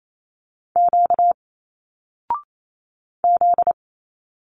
Morse code, heard as a keyed tone, about 28 wpm, 700 Hz. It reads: Q Z
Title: Single Letters 28wpm